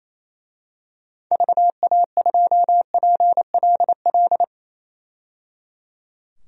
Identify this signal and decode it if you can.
Morse code, heard as a keyed tone, about 28 wpm, 700 Hz. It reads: VA2PLL
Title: DXCC-100 Call Signs - 6 Characters - 28wpm